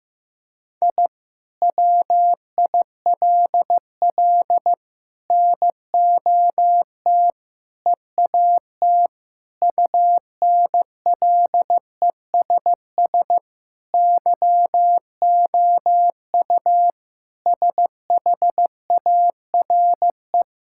Morse code, heard as a keyed tone, about 16 wpm, 700 Hz. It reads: I WILL NOT EAT UNLESS YOU SHARE
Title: Sentences from Top 1000 Words - Review Part 6 - 15wpm